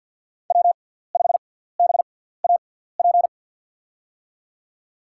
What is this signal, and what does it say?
Morse code, heard as a keyed tone, about 52 wpm, spaced as for 17 wpm, 700 Hz. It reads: W5BAP
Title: USA Call Signs - 6x Character Spacing - 50wpm